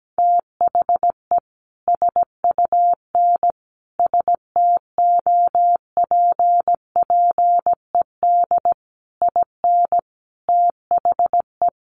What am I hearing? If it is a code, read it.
Morse code, heard as a keyed tone, about 17 wpm, 700 Hz. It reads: THE SUN STOPPED IN THE
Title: A Sound Of Thunder 17wpm